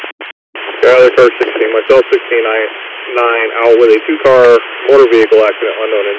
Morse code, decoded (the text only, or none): none